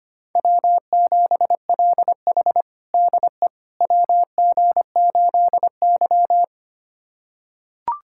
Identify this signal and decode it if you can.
Morse code, heard as a keyed tone, about 25 wpm, 700 Hz. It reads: W7LH DE WG8Y
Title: USA Callsign DE Callsign - 4 Characters - 25wpm